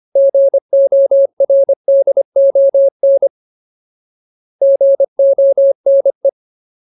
Morse code, heard as a keyed tone, about 25 wpm, 550 Hz. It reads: GORDON GONE
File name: Ultimate Word List / 25WPM Ultimate